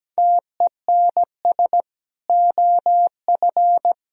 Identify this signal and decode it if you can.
Morse code, heard as a keyed tone, about 17 wpm, 700 Hz. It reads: TENS OF
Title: Speed-Racing - Sets of 4 Words 17wpm